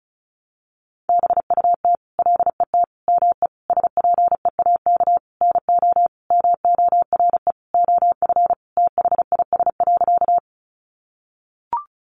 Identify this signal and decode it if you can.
Morse code, heard as a keyed tone, about 34 wpm, 700 Hz. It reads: BUT LET ME SPEAK NO MORE OF THIS.
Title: Sentences from Top 1100 Words 35wpm